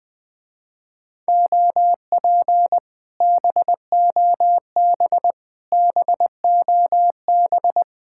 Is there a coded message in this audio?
Yes: Morse code from 1.3 to 7.8 s, heard as a keyed tone, about 20 wpm, 700 Hz. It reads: OP BOB BOB